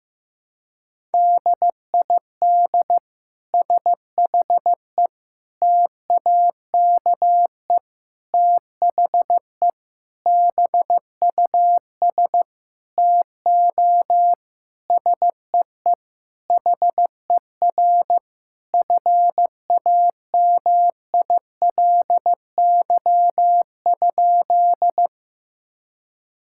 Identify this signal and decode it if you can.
Morse code, heard as a keyed tone, about 15 wpm, 700 Hz. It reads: DID SHE TAKE THE BUS TO SEE HER FAMILY?
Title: Sentences from Top 1000 Words - Review Part 6 - 15wpm